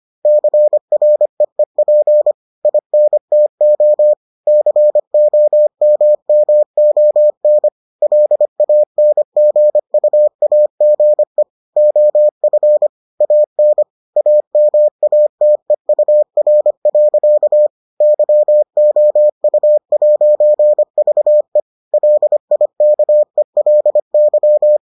Morse code, heard as a keyed tone, about 25 wpm, 600 Hz. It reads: CREEP INTO COMMON LANGUAGE OF AN AMATEUR. YOU'VE LIKELY